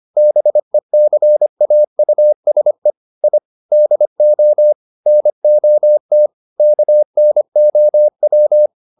Morse code, heard as a keyed tone, about 25 wpm, 600 Hz. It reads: BECAUSE I DO NOT KNOW
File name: the old man and the sea part 1 25wpm